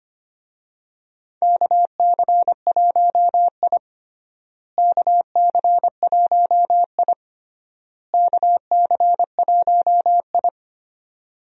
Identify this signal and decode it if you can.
Morse code, heard as a keyed tone, about 25 wpm, 700 Hz. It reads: KC1I KC1I KC1I
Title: USA Call Signs - 4 Characters - Repeated 3x - 25wpm